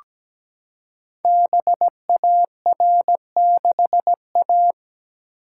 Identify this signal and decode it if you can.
Morse code, heard as a keyed tone, about 17 wpm, 700 Hz. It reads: BAR6A
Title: Code Groups - 5 Letter:Numbers - 17wpm